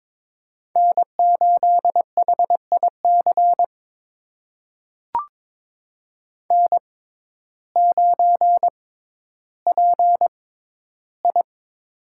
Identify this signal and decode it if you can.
Morse code, heard as a keyed tone, about 22 wpm, spaced as for 12 wpm, 700 Hz. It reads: N8HIC N9PI
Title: USA Call Signs - 6x Character Spacing - 22wpm